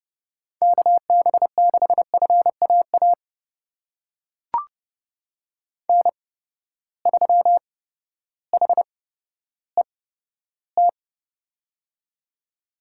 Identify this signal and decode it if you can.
Morse code, heard as a keyed tone, about 32 wpm, spaced as for 14 wpm, 700 Hz. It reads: KB6FAA N3HET
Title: USA Call Signs - 8x Character Spacing - 30wpm